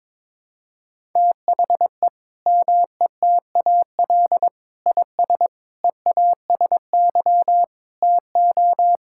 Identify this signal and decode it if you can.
Morse code, heard as a keyed tone, about 22 wpm, 700 Hz. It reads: THE METAL IS EASY TO
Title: Sentences from Top 1000 Words - Review Part 6 - 22wpm